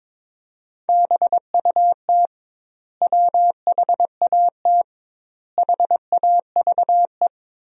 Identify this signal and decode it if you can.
Morse code, heard as a keyed tone, about 22 wpm, 700 Hz. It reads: BUT WHAT HAVE
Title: 2x Word-Spacing - Sentences from Top 100 Words - 22wpm